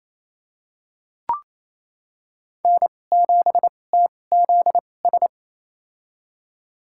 Morse code, heard as a keyed tone, about 28 wpm, spaced as for 20 wpm, 700 Hz. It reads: N7TZS
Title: USA Call Signs - 5 Characters - 2x Character Spacing - 28wpm